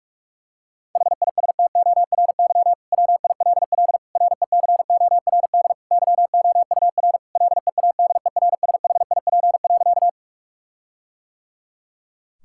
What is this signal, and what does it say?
Morse code, heard as a keyed tone, about 43 wpm, 700 Hz. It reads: HISTORY WILL RECORD YOUR LEADERSHIP.
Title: Sentences from Top 1600 Words 45wpm